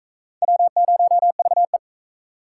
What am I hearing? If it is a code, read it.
Morse code, heard as a keyed tone, about 39 wpm, 700 Hz. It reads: W0VE